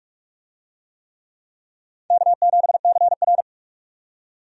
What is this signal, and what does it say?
Morse code, heard as a keyed tone, about 46 wpm, 700 Hz. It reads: K7CR
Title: USA Call Signs - 4 Characters - 45wpm